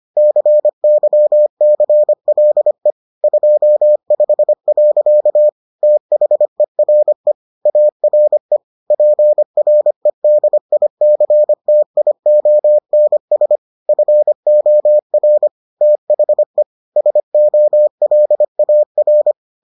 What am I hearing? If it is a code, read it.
Morse code, heard as a keyed tone, about 25 wpm, 600 Hz. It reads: CYCLE 25. THERE ARE PREDICTIONS FOR THE SOLAR